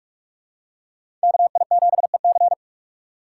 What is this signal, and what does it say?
Morse code, heard as a keyed tone, about 46 wpm, 700 Hz. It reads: KI7EC
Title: USA Call Signs - 6x Character Spacing - 45wpm